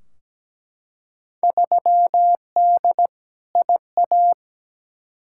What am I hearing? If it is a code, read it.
Morse code, heard as a keyed tone, about 17 wpm, 700 Hz. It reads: 3D IA
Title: Field Day Exchanges Encore - 17wpm